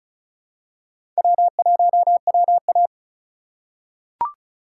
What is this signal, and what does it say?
Morse code, heard as a keyed tone, about 36 wpm, 700 Hz. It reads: W1WA
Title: USA Call Signs - 4 Characters - 4x Character Spacing - 35wpm